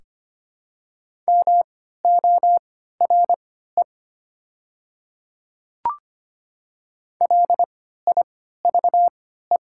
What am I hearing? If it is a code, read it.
Morse code, heard as a keyed tone, about 25 wpm, spaced as for 14 wpm, 700 Hz. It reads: MORE LIVE
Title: 3x Character Spacing - Top 200 Words - 25wpm